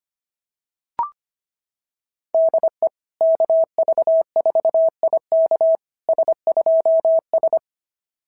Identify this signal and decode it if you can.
Morse code, heard as a keyed tone, about 24 wpm, 650 Hz. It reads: DE KV4IK S2S
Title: SOTA QSO Practice 25wpm